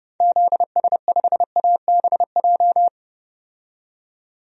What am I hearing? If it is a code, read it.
Morse code, heard as a keyed tone, about 31 wpm, 700 Hz. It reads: ZS5ABJ
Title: DXCC-100 Call Signs - 6 Characters - 30wpm